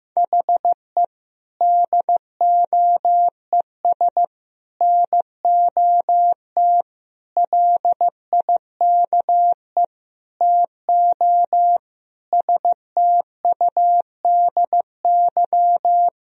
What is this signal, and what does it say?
Morse code, heard as a keyed tone, about 16 wpm, 700 Hz. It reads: HE DOES NOT LIKE TO STUDY
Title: Sentences from Top 1000 Words - Review Part 3 - 15wpm